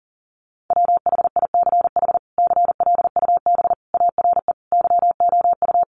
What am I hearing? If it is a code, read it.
Morse code, heard as a keyed tone, about 37 wpm, 700 Hz. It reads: WHICH CLUB ARE YOU
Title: Sentences from Top 1200 Words 40wpm